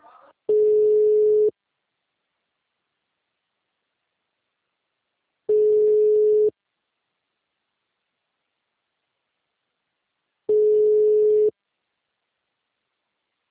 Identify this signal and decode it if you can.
Ringback tone (Europe, 425 Hz).
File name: out- -2003- - - .13478